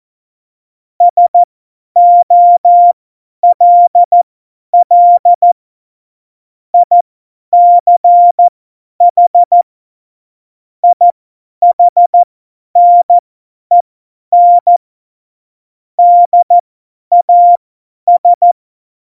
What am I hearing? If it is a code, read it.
Morse code, heard as a keyed tone, about 14 wpm, spaced as for 10 wpm, 700 Hz. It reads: SOLL ICH IHNEN DAS
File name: Morsekurs / Witze / Witz 25. Februar 2020, 50 F70